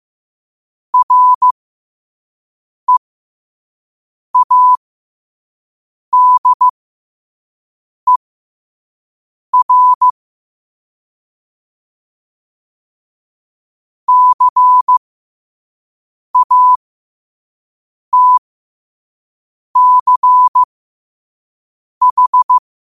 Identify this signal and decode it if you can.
Morse code, heard as a keyed tone, about 15 wpm, spaced as for 5 wpm, 1000 Hz. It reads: READER CATCH
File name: qotdfiles / 5wpm